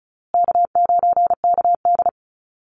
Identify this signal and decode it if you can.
Morse code, heard as a keyed tone, about 34 wpm, 700 Hz. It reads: K9KD
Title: USA Call Signs Repeated 3x 35wpm